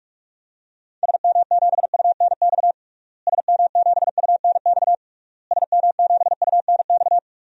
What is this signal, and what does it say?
Morse code, heard as a keyed tone, about 41 wpm, 700 Hz. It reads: SM7UNX SM7UNX SM7UNX
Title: DXCC-100 Call Signs - 6 Characters - Repeated 3x - 45wpm